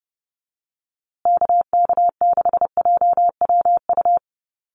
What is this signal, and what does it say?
Morse code, heard as a keyed tone, about 30 wpm, 700 Hz. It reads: KK6JWU
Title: USA Call Signs Repeated 3x 30wpm